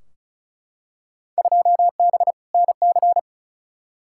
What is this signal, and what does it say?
Morse code, heard as a keyed tone, about 36 wpm, 690 Hz. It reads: 2B NC
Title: Field Day Exchanges Encore - 35wpm